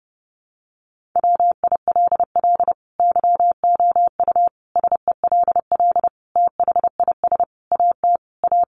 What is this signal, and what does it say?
Morse code, heard as a keyed tone, about 30 wpm, 700 Hz. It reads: WILL YOU SELL THIS AT A
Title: Sentences from Top 1200 Words 30wpm